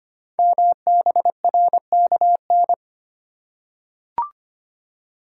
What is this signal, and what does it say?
Morse code, heard as a keyed tone, about 25 wpm, 700 Hz. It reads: MBRKN
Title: Code Groups - 5 Letters - 25wpm